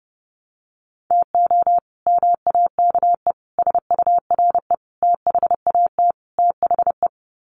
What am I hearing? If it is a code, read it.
Morse code, heard as a keyed tone, about 30 wpm, 700 Hz. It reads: TO MAKE SURE THAT THE
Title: Sets of 5 Words 30wpm